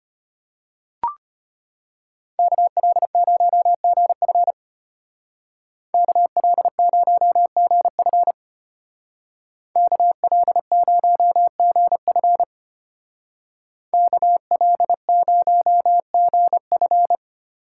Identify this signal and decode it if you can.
Morse code, heard as a keyed tone, about 32 wpm, 700 Hz. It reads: KL0GF KL0GF KL0GF KL0GF